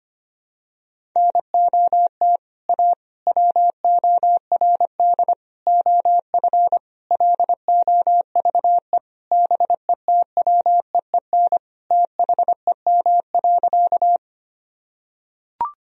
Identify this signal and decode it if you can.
Morse code, heard as a keyed tone, about 25 wpm, 700 Hz. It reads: NOT A WORD OF LOVE BETWEEN THEM.
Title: Sentences from Top 1000 Words - Review Part 1 - 25wpm